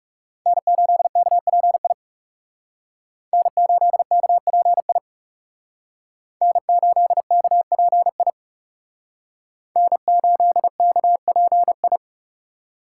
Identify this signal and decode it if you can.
Morse code, heard as a keyed tone, about 36 wpm, 700 Hz. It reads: N8KPI N8KPI N8KPI N8KPI
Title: Speed Racing - USA 1x3 Call Signs 30wpm